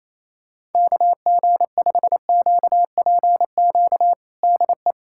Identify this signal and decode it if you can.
Morse code, heard as a keyed tone, about 28 wpm, 700 Hz. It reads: KG5QPQ DE
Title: USA Callsign DE Callsign - 6 Characters - 28wpm